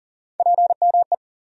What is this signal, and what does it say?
Morse code, heard as a keyed tone, about 40 wpm, 700 Hz. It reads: PME
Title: USA Call Sign Postfixes - 3 Letters 40wpm